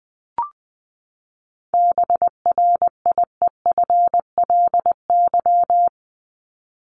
Morse code, heard as a keyed tone, about 20 wpm, 700 Hz. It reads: BRIEFLY
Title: LY Suffix - Top 100 Words with Base Words 20wpm